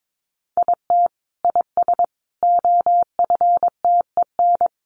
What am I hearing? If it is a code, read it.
Morse code, heard as a keyed tone, about 22 wpm, 700 Hz. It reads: IT IS OFTEN
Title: Sentences from Top 1400 Words 22wpm